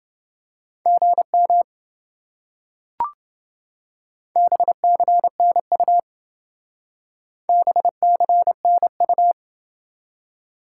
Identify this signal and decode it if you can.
Morse code, heard as a keyed tone, about 28 wpm, 700 Hz. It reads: GM BCNU BCNU